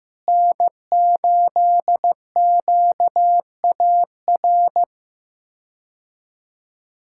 Morse code, heard as a keyed tone, about 15 wpm, 700 Hz. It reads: N8QAR
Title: K1USN SST QSO Practice - 15wpm